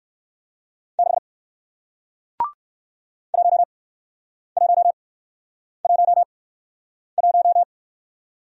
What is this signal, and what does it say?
Morse code, heard as a keyed tone, about 60 wpm, spaced as for 9 wpm, 700 Hz. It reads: 6 1111